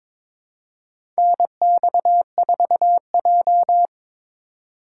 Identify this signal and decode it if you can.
Morse code, heard as a keyed tone, about 23 wpm, 700 Hz. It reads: NX4J
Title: USA Call Signs - 4 Characters - 2x Character Spacing - 22wpm